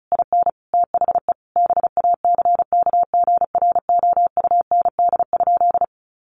Morse code, heard as a keyed tone, about 34 wpm, 700 Hz. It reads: IN THE BACKGROUND?
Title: Sentences from Top 1300 Words 35wpm